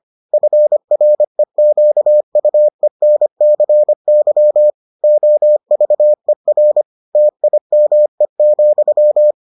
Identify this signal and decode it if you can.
Morse code, heard as a keyed tone, about 25 wpm, 600 Hz. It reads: FREQUENCY OVER TIME,